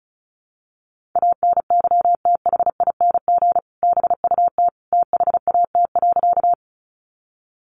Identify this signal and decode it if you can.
Morse code, heard as a keyed tone, about 34 wpm, 700 Hz. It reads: ANYTHING BUT THAT.
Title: Sentences from Top 1000 Words 35wpm